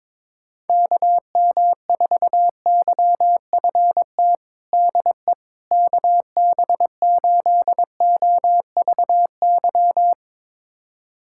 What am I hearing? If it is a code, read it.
Morse code, heard as a keyed tone, about 21 wpm, 700 Hz. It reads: KM4YFT DE KB8OVY
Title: USA Callsign DE Callsign - 6 Characters - 22wpm